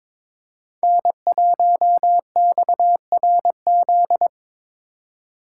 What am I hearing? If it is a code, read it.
Morse code, heard as a keyed tone, about 26 wpm, 700 Hz. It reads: N1XRZ
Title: Speed Racing - USA 1x3 Call Signs 22wpm